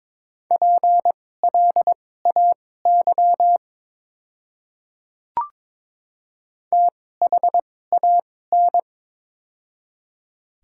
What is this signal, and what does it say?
Morse code, heard as a keyed tone, about 22 wpm, spaced as for 16 wpm, 700 Hz. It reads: PLAY THAN
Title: 2x Character Spacing - Top 200 Words - 22wpm